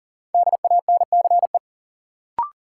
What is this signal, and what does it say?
Morse code, heard as a keyed tone, about 40 wpm, 700 Hz. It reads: DANCE
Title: Sentences from Top 700 Words - New Word Drill - 40wpm